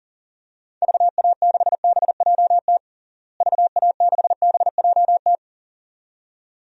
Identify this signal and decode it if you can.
Morse code, heard as a keyed tone, about 40 wpm, 700 Hz. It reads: VA6BJT VA6BJT
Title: DXCC-100 Call Signs - 6 Characters - Repeated 2x - 40wpm